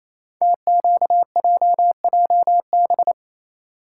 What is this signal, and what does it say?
Morse code, heard as a keyed tone, about 27 wpm, 700 Hz. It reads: TQJJB
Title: Code Groups - 5 Letter:Numbers - 28wpm